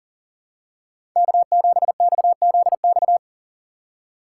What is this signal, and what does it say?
Morse code, heard as a keyed tone, about 40 wpm, 700 Hz. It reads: K7XZX
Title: USA Call Signs - 4x Character Spacing - 40wpm